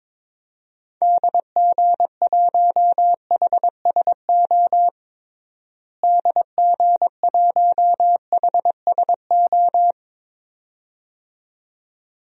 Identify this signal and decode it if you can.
Morse code, heard as a keyed tone, about 22 wpm, 700 Hz. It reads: DG1HSO DG1HSO
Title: DXCC-100 Call Signs - 6 Characters - Repeated 2x - 22wpm